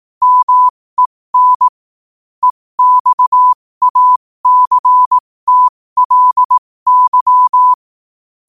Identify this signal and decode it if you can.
Morse code, heard as a keyed tone, about 18 wpm, spaced as for 15 wpm, 1000 Hz. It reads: MEN EXACTLY
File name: qotdfiles / 15wpm